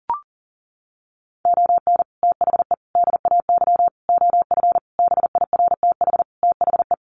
Morse code, heard as a keyed tone, about 40 wpm, 700 Hz. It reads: ON THE DAY OF BIRTH THE
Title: Sentences from Top 1100 Words 40wpm